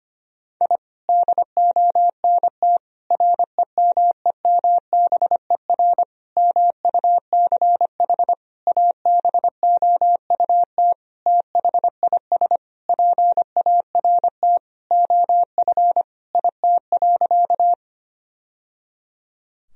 Morse code, heard as a keyed tone, about 25 wpm, 700 Hz. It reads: I DONT REMEMBER MUCH ABOUT THIS PART OF IT.
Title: Sentences from Top 1000 Words - Review Part 1 - 25wpm